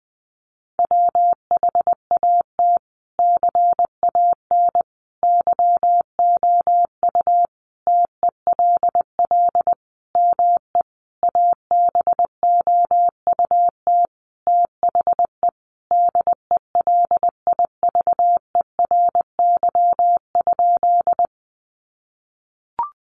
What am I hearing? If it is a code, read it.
Morse code, heard as a keyed tone, about 20 wpm, 700 Hz. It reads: WHAT CAN YOU TELL ME ABOUT THE DELIVERY?
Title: Sentences from Top 1500 Words 20wpm